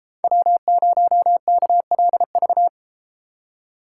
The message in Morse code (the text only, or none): W0KLV